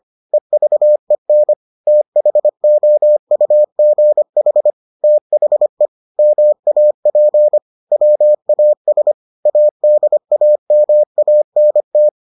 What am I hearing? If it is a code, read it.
Morse code, heard as a keyed tone, about 25 wpm, 600 Hz. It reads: EVEN THOUGH THE MAP WAS ADAMANT